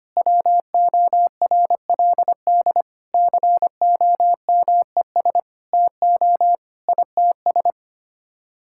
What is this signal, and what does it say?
Morse code, heard as a keyed tone, about 25 wpm, 700 Hz. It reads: WORLD COMES TO ITS
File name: Sets of 9 Words - 25wpm